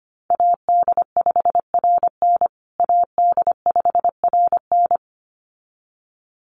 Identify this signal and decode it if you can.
Morse code, heard as a keyed tone, about 25 wpm, 700 Hz. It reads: AD5RN AD5RN
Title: USA Call Signs Repeated 2x 25wpm